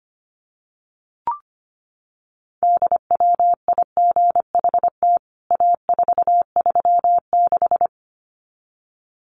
Morse code, heard as a keyed tone, about 25 wpm, 700 Hz. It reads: DWIGHT A436
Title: CWT Exchange 25wpm